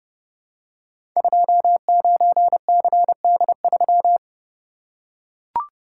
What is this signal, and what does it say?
Morse code, heard as a keyed tone, about 30 wpm, 700 Hz. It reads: 29CD3
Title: Code Groups - 5 Letter:Numbers - 30wpm